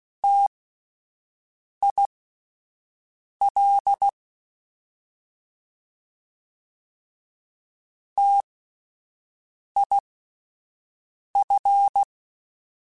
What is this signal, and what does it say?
Morse code, heard as a keyed tone, about 16 wpm, spaced as for 5 wpm, 800 Hz. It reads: TIL TIF
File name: Spaced Alphas / Spaced Alphas mp3s / Ts sp @10